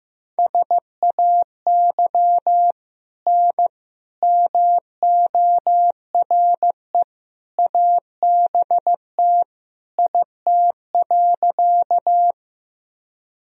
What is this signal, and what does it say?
Morse code, heard as a keyed tone, about 15 wpm, 700 Hz. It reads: SAY N MORE ABT IT.
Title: Sentences from Top 200 Words and Abbreviations 15wpm